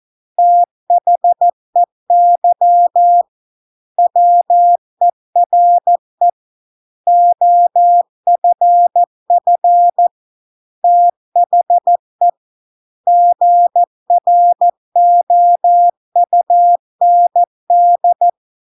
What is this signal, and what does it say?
Morse code, heard as a keyed tone, about 14 wpm, 700 Hz. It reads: THEY WERE OFF THE GROUND